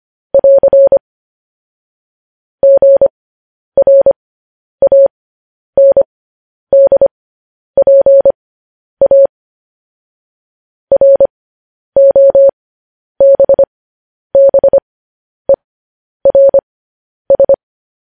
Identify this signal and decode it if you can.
Morse code, heard as a keyed tone, about 25 wpm, spaced as for 10 wpm, 550 Hz. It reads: <AR> GRANDPA ROBBERS
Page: news headlines in morse code at 10 wpm